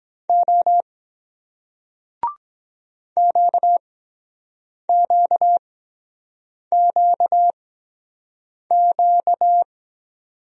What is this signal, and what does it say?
Morse code, heard as a keyed tone, about 22 wpm, spaced as for 7 wpm, 700 Hz. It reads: O QQQQ